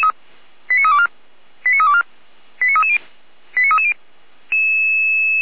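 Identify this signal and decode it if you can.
Five-tone selective call (ZVEI), heard as quick runs of five tones, tones of about 70 ms. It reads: 89324 89324 89330 89330
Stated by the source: ZVEI five-tone selective call received via SDR, from the feuerwehr-orlamuende.org server (Feuerwehr Orlamünde, fire brigade).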